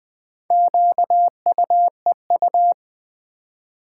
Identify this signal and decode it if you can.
Morse code, heard as a keyed tone, about 20 wpm, 700 Hz. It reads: QUEU
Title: Build A Word - 1 Letter At A Time - 3 to 5-Letter Words - 20wpm